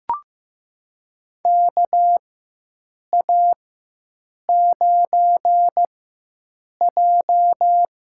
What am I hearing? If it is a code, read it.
Morse code, heard as a keyed tone, about 15 wpm, spaced as for 7 wpm, 700 Hz. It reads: KA9J
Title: USA Call Signs - 4x Character Spacing - 15wpm